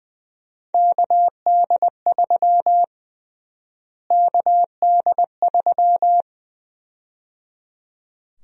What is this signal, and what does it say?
Morse code, heard as a keyed tone, about 21 wpm, 700 Hz. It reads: KD3 KD3
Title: USA Call Sign Prefixes - Repeated 2x - 20wpm